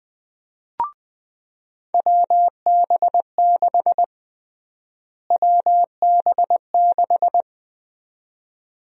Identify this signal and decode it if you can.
Morse code, heard as a keyed tone, about 20 wpm, 700 Hz. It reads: WB6 WB6
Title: USA Call Sign Prefixes - Repeated 2x - 20wpm